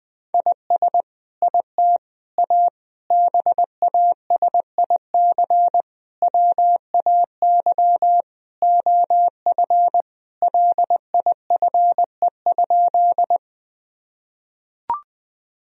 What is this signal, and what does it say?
Morse code, heard as a keyed tone, about 20 wpm, 700 Hz. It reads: IS IT A BASIC WAY OF LIFE?
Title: Sentences from Top 1000 Words - Review Part 3 - 20wpm